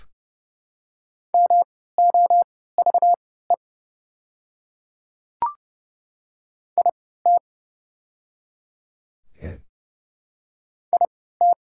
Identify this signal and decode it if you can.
Morse code, heard as a keyed tone, about 30 wpm, spaced as for 17 wpm, 700 Hz. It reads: MOVE IT IT
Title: 3x Character Spacing - Top 200 Words - 30wpm